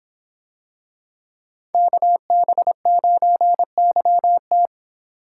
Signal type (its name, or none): Morse code